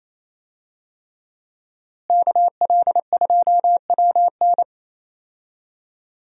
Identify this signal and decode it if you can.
Morse code, heard as a keyed tone, about 28 wpm, 700 Hz. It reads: KL2WN
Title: USA Call Signs - 5 Characters - Repeated 3x - 28wpm